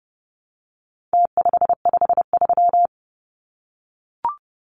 Morse code, heard as a keyed tone, about 30 wpm, 700 Hz. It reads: T553
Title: Numbers - Contest Style - 4-Digit 30wpm